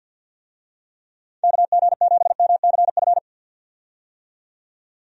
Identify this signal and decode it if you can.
Morse code, heard as a keyed tone, about 48 wpm, 700 Hz. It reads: KG7MCF
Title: USA Call Signs - 6 Characters - 50wpm